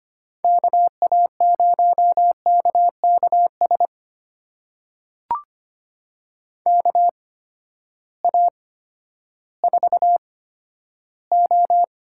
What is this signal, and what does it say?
Morse code, heard as a keyed tone, about 25 wpm, spaced as for 12 wpm, 700 Hz. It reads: KA0KKS KA4O
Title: USA Call Signs - 8x Character Spacing - 25wpm